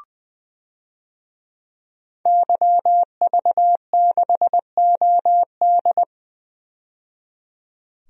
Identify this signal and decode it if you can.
Morse code, heard as a keyed tone, about 21 wpm, 700 Hz. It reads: YV6OD
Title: DXCC-100 Call Signs - 5 Characters - 20wpm